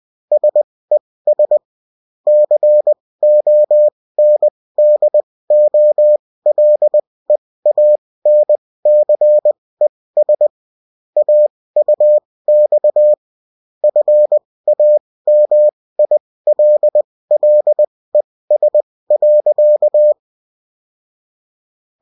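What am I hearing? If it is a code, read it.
Morse code, heard as a keyed tone, about 20 wpm, spaced as for 16 wpm, 600 Hz. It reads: SES CONDOLEANCES AUX FAMILLES.
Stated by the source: Morse code bulletin of the UFT.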